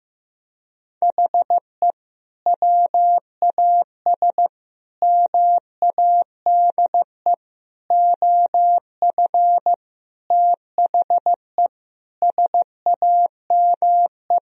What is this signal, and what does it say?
Morse code, heard as a keyed tone, about 15 wpm, 700 Hz. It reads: HE WAS MADE OF THE SAME
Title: Sentences from Top 1000 Words - Review Part 4 - 15wpm